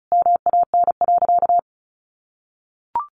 Morse code, heard as a keyed tone, about 34 wpm, 700 Hz. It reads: MAN.
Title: Sentences from Top 600 Words 35wpm